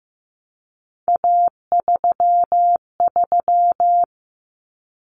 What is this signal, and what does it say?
Morse code, heard as a keyed tone, about 15 wpm, 700 Hz. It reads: A33
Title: Common Cut Numbers - 3-Digit 15wpm